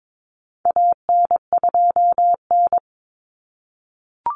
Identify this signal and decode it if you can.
Morse code, heard as a keyed tone, about 22 wpm, 700 Hz. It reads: AN2N
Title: Common Cut Numbers - 4-Digit 22wpm